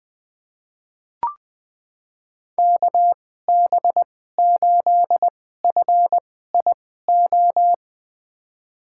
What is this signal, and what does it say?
Morse code, heard as a keyed tone, about 20 wpm, spaced as for 14 wpm, 700 Hz. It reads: KB8FIO
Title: USA Call Signs - 2x Character Spacing - 20wpm